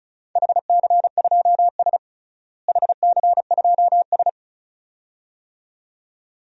Morse code, heard as a keyed tone, about 36 wpm, 700 Hz. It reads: HC2S HC2S
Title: DXCC-100 Call Signs - 4 Characters - Repeated 2x - 35wpm